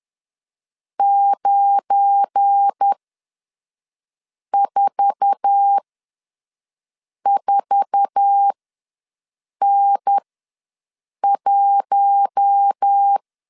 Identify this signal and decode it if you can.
Morse code, heard as a keyed tone, about 10 wpm, spaced as for 5 wpm, 800 Hz. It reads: 944N1
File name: LEZIONI / 8 YJ45 / 8 YJ4 ws 10wpm 4020